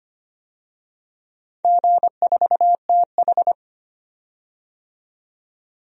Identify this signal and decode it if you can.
Morse code, heard as a keyed tone, about 25 wpm, 700 Hz. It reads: G4TH
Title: DXCC-100 Call Signs - 4 Characters - 25wpm